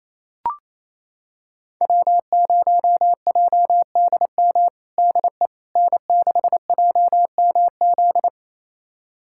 Morse code, heard as a keyed tone, about 28 wpm, 700 Hz. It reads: W0JDM DE N6JMZ
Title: USA Callsign DE Callsign 28wpm